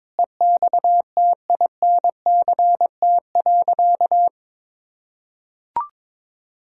Morse code, heard as a keyed tone, about 22 wpm, 700 Hz. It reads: EXTINCT.
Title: Pangram Challenge v2 - 22wpm